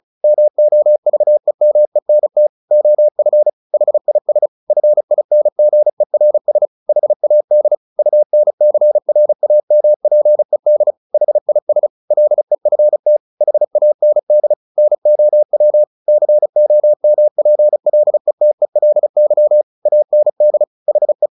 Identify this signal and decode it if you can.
Morse code, heard as a keyed tone, about 34 wpm, 600 Hz. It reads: MOVEMENT OF HIS FINGERS HAD UNCRAMPED HIS LEFT HAND NOW COMPLETELY AND HE
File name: the old man and the sea / the old man and the sea 1 35wpm